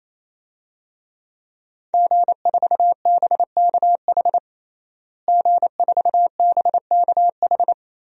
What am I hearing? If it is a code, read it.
Morse code, heard as a keyed tone, about 27 wpm, 700 Hz. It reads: G4BKH G4BKH
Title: DXCC-100 Call Signs - 5 Characters - Repeated 3x - 28wpm